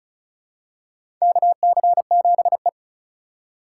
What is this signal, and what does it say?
Morse code, heard as a keyed tone, about 34 wpm, 700 Hz. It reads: KC7E
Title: USA Call Signs - 4x Character Spacing - 35wpm